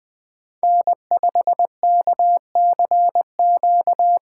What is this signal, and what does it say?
Morse code, heard as a keyed tone, about 20 wpm, 700 Hz. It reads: N5KCQ
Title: USA Call Signs - 5 Characters - 8x Character Spacing - 20wpm